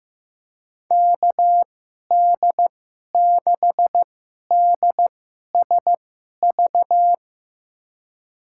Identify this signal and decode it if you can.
Morse code, heard as a keyed tone, about 15 wpm, spaced as for 11 wpm, 700 Hz. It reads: KD6DSV
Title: USA Call Signs - 2x Character Spacing - 15wpm